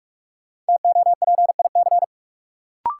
Morse code, heard as a keyed tone, about 41 wpm, 700 Hz. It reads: TOPIC
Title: Sentences from Top 1200 Words - New Word Drill Alt - 45wpm